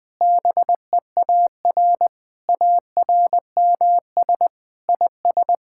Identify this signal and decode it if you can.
Morse code, heard as a keyed tone, about 20 wpm, 700 Hz. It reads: BEAR ARMS IS
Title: Sentences with 7-Word Phrases - 20wpm